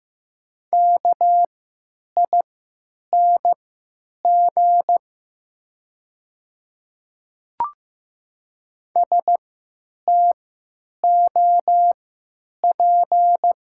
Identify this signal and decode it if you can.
Morse code, heard as a keyed tone, about 15 wpm, spaced as for 9 wpm, 700 Hz. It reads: KING STOP
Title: 3x Character Spacing - Top 500 Words - 15wpm